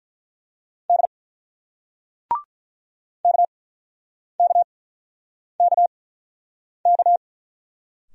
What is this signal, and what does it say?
Morse code, heard as a keyed tone, about 47 wpm, spaced as for 9 wpm, 700 Hz. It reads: D KKKK